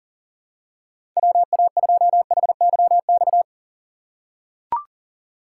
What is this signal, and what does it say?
Morse code, heard as a keyed tone, about 40 wpm, 700 Hz. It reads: WA2HYX
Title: USA Call Signs - 6 Characters - 3x Character Spacing - 40wpm